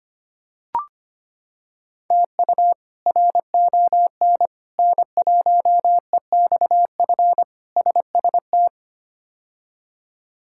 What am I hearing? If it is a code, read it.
Morse code, heard as a keyed tone, about 25 wpm, 700 Hz. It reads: TU RON N1EXF SST
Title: K1USN SST QSO Practice - 25wpm